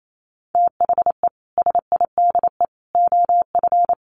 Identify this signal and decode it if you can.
Morse code, heard as a keyed tone, about 28 wpm, 700 Hz. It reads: THE SIDE OF
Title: Sets of 3 Words 28wpm